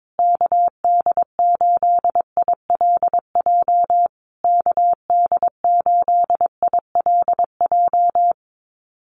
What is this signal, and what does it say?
Morse code, heard as a keyed tone, about 22 wpm, 700 Hz. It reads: KD8ILJ KD8ILJ
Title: USA Call Signs Repeated 3x 22wpm